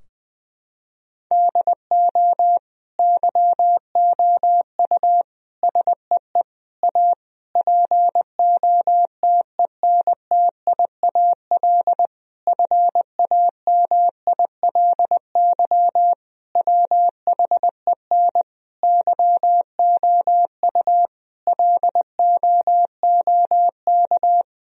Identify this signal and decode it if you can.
Morse code, heard as a keyed tone, about 21 wpm, 700 Hz. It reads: DO YOU SEE A POTENTIAL FAMILY WHEN YOU LOOK
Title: Sentences from Top 1000 Words - Review Part 4 - 20wpm